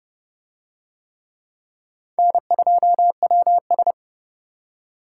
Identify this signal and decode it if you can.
Morse code, heard as a keyed tone, about 26 wpm, 700 Hz. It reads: N2WS